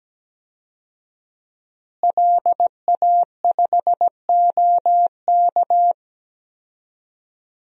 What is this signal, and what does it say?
Morse code, heard as a keyed tone, about 17 wpm, 700 Hz. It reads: LA5OK
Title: DXCC-100 Call Signs - 5 Characters - Repeated 3x - 17wpm